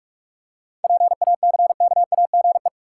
Morse code, heard as a keyed tone, about 47 wpm, 700 Hz. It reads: PACKAGE
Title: Seven Letter Words - 45wpm